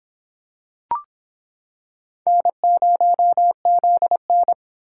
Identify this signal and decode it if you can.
Morse code, heard as a keyed tone, about 26 wpm, 700 Hz. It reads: N0ZN